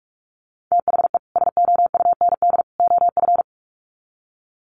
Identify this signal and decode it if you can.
Morse code, heard as a keyed tone, about 44 wpm, 700 Hz. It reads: THE SOUND OF
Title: Sets of 3 Words 45wpm